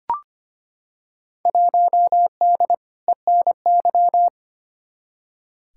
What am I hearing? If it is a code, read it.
Morse code, heard as a keyed tone, about 25 wpm, 700 Hz. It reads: 1D ENY
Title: Field Day Exchanges - 25wpm